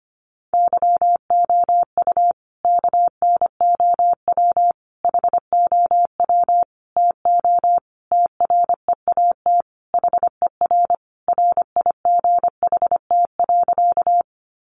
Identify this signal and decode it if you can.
Morse code, heard as a keyed tone, about 25 wpm, 700 Hz. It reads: YOU KNOW HOW TO TREAT HER RIGHT.
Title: Sentences from Top 1300 Words 25wpm